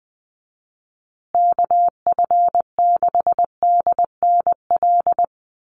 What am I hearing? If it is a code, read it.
Morse code, heard as a keyed tone, about 20 wpm, 700 Hz. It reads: KF6DNL
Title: USA Call Signs Repeated 3x 20wpm